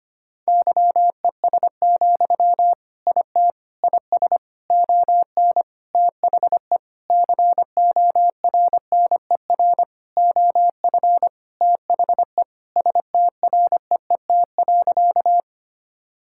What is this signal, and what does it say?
Morse code, heard as a keyed tone, about 25 wpm, 700 Hz. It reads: YES, IT IS ON THE CORNER OF THE STREET.
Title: Sentences from Top 1600 Words 25wpm